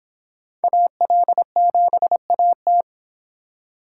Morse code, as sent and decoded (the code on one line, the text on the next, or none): .- .-.. --... .- -
AL7AT